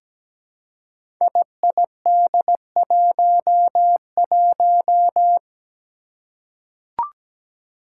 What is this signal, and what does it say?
Morse code, heard as a keyed tone, about 17 wpm, 700 Hz. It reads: IID11
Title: Five Challenging Letter-Number Combinations 17wpm